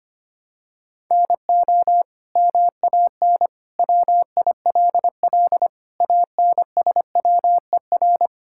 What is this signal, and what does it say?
Morse code, heard as a keyed tone, about 25 wpm, 700 Hz. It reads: NO MAN WILL ANSWER
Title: Sentences from Top 1000 Words - Review Part 6 - 25wpm